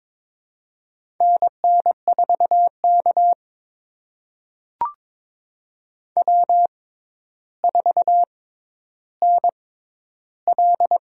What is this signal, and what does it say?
Morse code, heard as a keyed tone, about 21 wpm, spaced as for 11 wpm, 700 Hz. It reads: NN4K W4NL
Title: USA Call Signs - 4 Characters - 6x Character Spacing - 22wpm